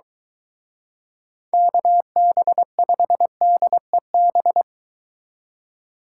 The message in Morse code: KB5DEB

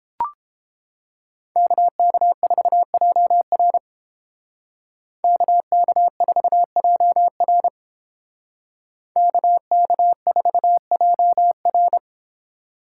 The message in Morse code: KK4JR KK4JR KK4JR

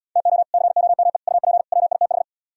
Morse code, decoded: TO COME UP WITH

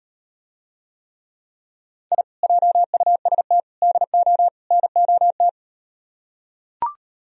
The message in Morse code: I JUST DO NOT

